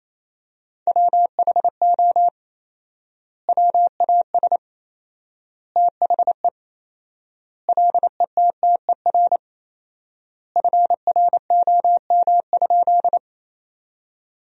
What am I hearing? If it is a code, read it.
Morse code, heard as a keyed tone, about 28 wpm, 700 Hz. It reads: WHO WAS THE LETTER FROM?
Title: 4x Word-Spacing - Sentences from Top 300 Words - 28wpm